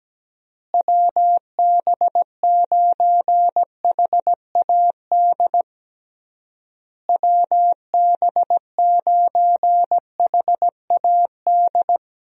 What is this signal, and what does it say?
Morse code, heard as a keyed tone, about 17 wpm, 700 Hz. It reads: WB9HAD WB9HAD
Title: USA Call Signs - 6 Characters - Repeated 3x - 17wpm